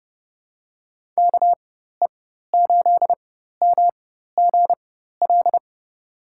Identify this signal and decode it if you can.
Morse code, heard as a keyed tone, about 31 wpm, spaced as for 14 wpm, 700 Hz. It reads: KE8MGL
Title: USA Call Signs - 6 Characters - 4x Character Spacing - 30wpm